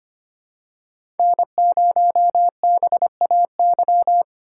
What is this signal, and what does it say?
Morse code, heard as a keyed tone, about 25 wpm, 700 Hz. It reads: N0BAY
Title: USA Call Signs - 5 Characters - 2x Character Spacing - 25wpm